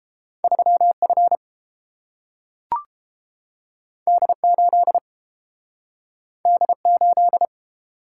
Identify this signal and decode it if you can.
Morse code, heard as a keyed tone, about 30 wpm, 700 Hz. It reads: 3 F D 8 D 8